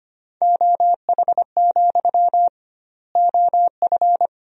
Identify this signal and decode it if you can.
Morse code, heard as a keyed tone, about 25 wpm, 700 Hz. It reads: OH, OF
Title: 2x Word-Spacing - Sentences from Top 500 Words - 25wpm